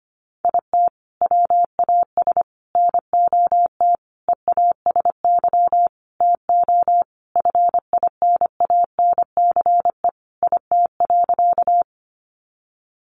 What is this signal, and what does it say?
Morse code, heard as a keyed tone, about 25 wpm, 700 Hz. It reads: IT WAS NOT EASY TO FINANCE IT.
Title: Sentences from Top 1500 Words 25wpm